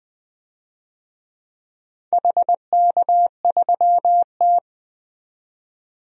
Morse code, heard as a keyed tone, about 20 wpm, 700 Hz. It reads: HK3T
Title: DXCC-100 Call Signs - 4 Characters - 20wpm